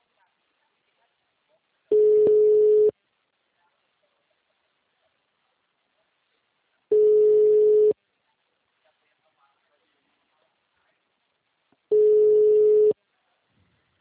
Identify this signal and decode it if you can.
Ringback tone (Europe, 425 Hz), with speech.
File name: out- -2022- - - .54684